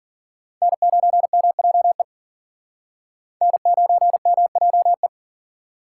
Morse code, heard as a keyed tone, about 54 wpm, 700 Hz. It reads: N9MJE N9MJE